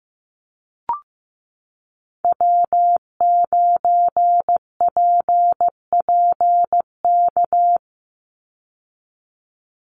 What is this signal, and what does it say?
Morse code, heard as a keyed tone, about 15 wpm, 700 Hz. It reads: W9PPK
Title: USA Call Signs 15wpm